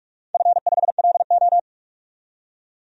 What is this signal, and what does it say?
Morse code, heard as a keyed tone, about 44 wpm, 700 Hz. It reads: F5LO
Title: DXCC-100 Call Signs - 4 Characters - 45wpm